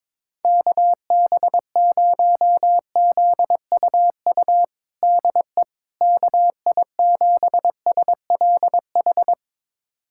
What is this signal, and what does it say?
Morse code, heard as a keyed tone, about 21 wpm, 700 Hz. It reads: KB0ZUU DE KI7SLH
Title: USA Callsign DE Callsign - 6 Characters - 22wpm